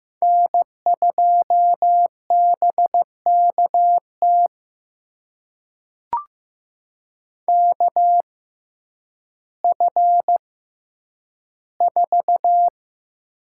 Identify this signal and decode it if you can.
Morse code, heard as a keyed tone, about 15 wpm, spaced as for 9 wpm, 700 Hz. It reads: N2BKT KF4
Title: USA Call Signs - 6x Character Spacing - 15wpm